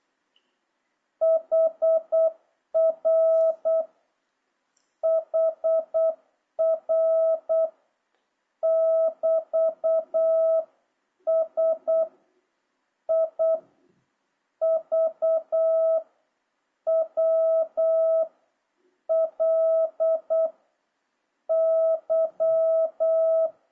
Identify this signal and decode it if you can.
Morse code, heard as a keyed tone, about 8 wpm, 650 Hz. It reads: HR HR <BT> S I V W L Y
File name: jh8cbh / 2026cw